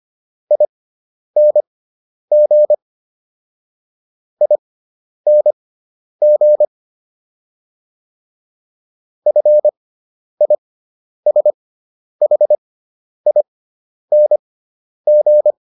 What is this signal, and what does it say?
Morse code, heard as a keyed tone, about 25 wpm, spaced as for 10 wpm, 600 Hz. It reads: ING ING FISHING